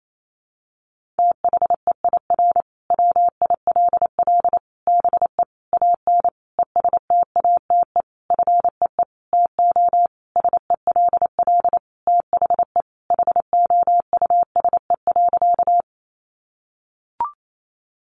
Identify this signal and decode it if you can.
Morse code, heard as a keyed tone, about 28 wpm, 700 Hz. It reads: THEIR WILL BE AN ESTATE FEE TO SELL THE HOUSE.
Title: Sentences from Top 1500 Words 28wpm